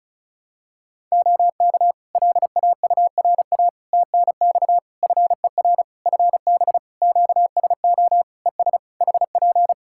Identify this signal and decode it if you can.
Morse code, heard as a keyed tone, about 36 wpm, 700 Hz. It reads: OK LAURA TNX FER FB QSO ES HP
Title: K1USN SST QSO Practice 35wpm